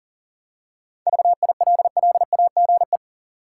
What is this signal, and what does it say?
Morse code, heard as a keyed tone, about 40 wpm, 700 Hz. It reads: VILLAGE
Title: Seven Letter Words - 40wpm